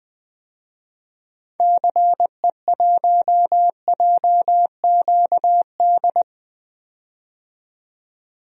Morse code, heard as a keyed tone, about 20 wpm, 700 Hz. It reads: CE1JQD
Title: DXCC-100 Call Signs - 6 Characters - 20wpm